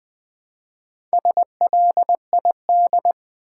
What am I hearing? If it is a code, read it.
Morse code, heard as a keyed tone, about 20 wpm, 700 Hz. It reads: SLID
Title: Build A Word - 1 Letter At A Time - 3 to 5-Letter Words - 20wpm